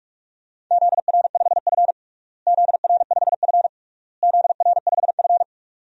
Morse code, heard as a keyed tone, about 47 wpm, 700 Hz. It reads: ZR5F ZR5F ZR5F
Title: DXCC-100 Call Signs - 4 Characters - Repeated 3x - 45wpm